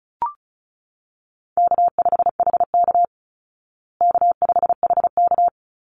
Morse code, heard as a keyed tone, about 34 wpm, 700 Hz. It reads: K5HK K5HK
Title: Easy Call Signs Repeated 2x - 35wpm